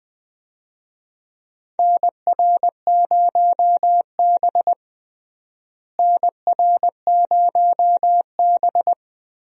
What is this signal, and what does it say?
Morse code, heard as a keyed tone, about 20 wpm, 700 Hz. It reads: NR0B NR0B
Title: USA Call Signs - 4 Characters - Repeated 3x - 20wpm